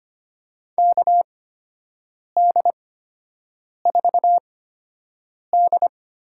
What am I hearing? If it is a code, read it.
Morse code, heard as a keyed tone, about 25 wpm, spaced as for 7 wpm, 700 Hz. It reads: KD4D
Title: USA Call Signs - 8x Character Spacing - 25wpm